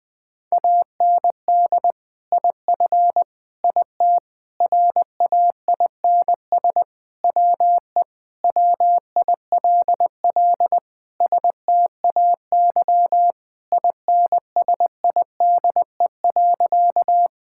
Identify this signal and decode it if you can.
Morse code, heard as a keyed tone, about 20 wpm, 700 Hz. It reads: AND IF IT RAINS WE WILL STAY INSIDE.
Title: Sentences with 2-Word Phrases - Spoken Beginning - 20wpm